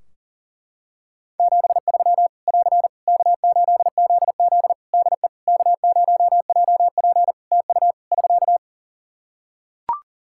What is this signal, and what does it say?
Morse code, heard as a keyed tone, about 42 wpm, 700 Hz. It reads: 73 <AR> K8ZZ DE K0JP TU <SK>
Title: Practice QSO - Exchange 4 - Ending - 40wpm